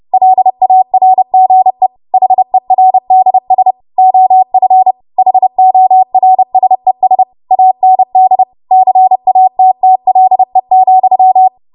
Morse code, heard as a keyed tone, about 30 wpm, 750 Hz. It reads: LARGE HERDS OF HORSES AND CATTLE,
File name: Morse / 30WPM